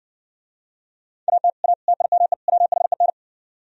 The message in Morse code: AT A TIME WHEN

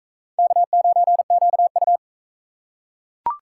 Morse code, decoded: K9QU

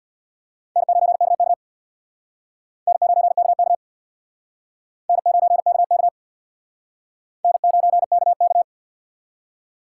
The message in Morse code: N9KK N9KK N9KK N9KK

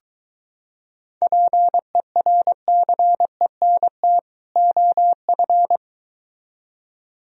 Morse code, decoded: PERCENT OF

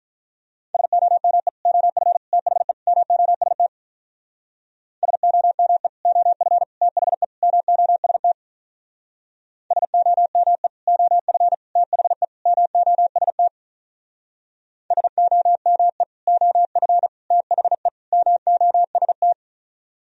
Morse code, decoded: SOME OF THE MOST SOME OF THE MOST SOME OF THE MOST SOME OF THE MOST

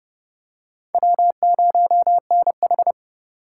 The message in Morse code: W0NH

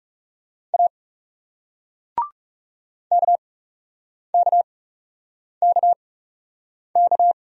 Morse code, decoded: A KKKK

